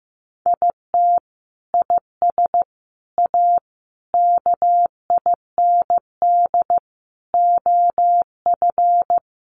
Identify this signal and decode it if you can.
Morse code, heard as a keyed tone, about 15 wpm, 700 Hz. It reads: IT IS A KIND OF
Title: Sentences from Top 1200 Words 15wpm